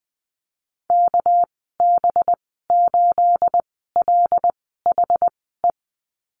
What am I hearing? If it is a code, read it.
Morse code, heard as a keyed tone, about 20 wpm, spaced as for 14 wpm, 700 Hz. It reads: KB8LHE
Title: 231 Lesson 37g - Letter x - Call Signs 20wpm